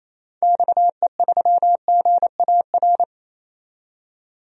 Morse code, heard as a keyed tone, about 27 wpm, 700 Hz. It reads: XE3GAR
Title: DXCC-100 Call Signs - 6 Characters - 28wpm